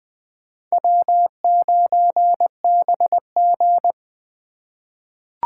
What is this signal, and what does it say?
Morse code, heard as a keyed tone, about 20 wpm, 700 Hz. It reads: W9BG
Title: USA Call Signs - 4 Characters - 3x Character Spacing - 20wpm